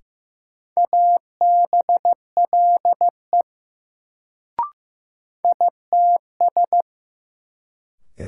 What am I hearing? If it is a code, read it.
Morse code, heard as a keyed tone, about 16 wpm, 700 Hz. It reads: ABLE ITS
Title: Sets of 2 Words - New Word Drill - 15wpm